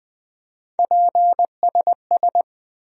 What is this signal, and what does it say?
Morse code, heard as a keyed tone, about 20 wpm, 700 Hz. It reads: PSS
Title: USA Call Sign Postfixes - 3 Letters 20wpm